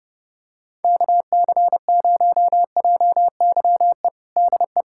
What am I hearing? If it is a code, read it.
Morse code, heard as a keyed tone, about 30 wpm, 700 Hz. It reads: KC0JYE DE
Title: USA Callsign DE Callsign - 6 Characters - 30wpm